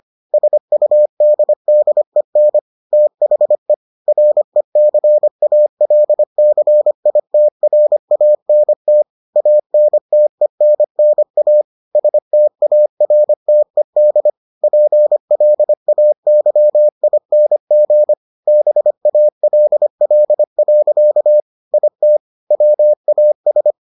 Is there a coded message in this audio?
Yes: Morse code from 0.3 to 23.7 s, heard as a keyed tone, about 25 wpm, 600 Hz. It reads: SUDDEN THE RECALCITRANT ANTENNA STARTED PLAYING BALL. IT WAS